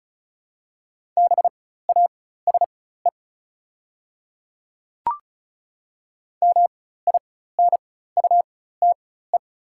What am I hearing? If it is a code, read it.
Morse code, heard as a keyed tone, about 36 wpm, spaced as for 16 wpm, 690 Hz. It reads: BASE MINUTE
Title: 4x Character Spacing - Top 500 Words - 35wpm